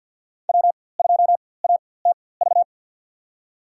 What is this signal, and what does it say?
Morse code, heard as a keyed tone, about 51 wpm, spaced as for 23 wpm, 700 Hz. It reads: W2ATV
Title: USA Call Signs - 5 Characters - 4x Character Spacing - 50wpm